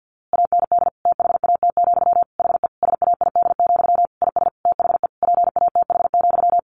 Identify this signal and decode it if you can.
Morse code, heard as a keyed tone, about 50 wpm, 700 Hz. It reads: AND THAT, HE SAID, IS THE PATH,
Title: A Sound Of Thunder 50wpm